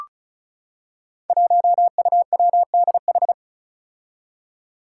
Morse code, heard as a keyed tone, about 35 wpm, 700 Hz. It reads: 1UWDH
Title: Five Challenging Letter-Number Combinations 35wpm